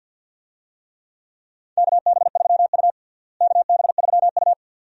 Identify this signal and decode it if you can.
Morse code, heard as a keyed tone, about 49 wpm, 700 Hz. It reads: KB3U KB3U
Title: USA Call Signs - 4 Characters - Repeated 2x - 50wpm